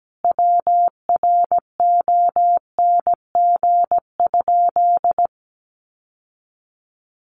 Morse code, heard as a keyed tone, about 17 wpm, 700 Hz. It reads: WRONG?
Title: Sentences from Top 1000 Words 17wpm